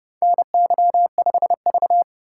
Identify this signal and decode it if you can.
Morse code, heard as a keyed tone, about 30 wpm, 700 Hz. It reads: NY5V
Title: USA Call Signs - 4 Characters - 4x Character Spacing - 30wpm